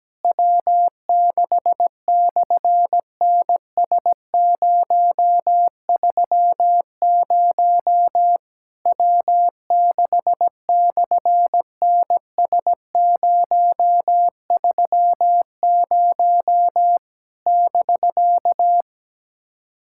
Morse code, heard as a keyed tone, about 17 wpm, 700 Hz. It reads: W6/NS030 W6/NS030 <BK>
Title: SOTA QSO Practice 17wpm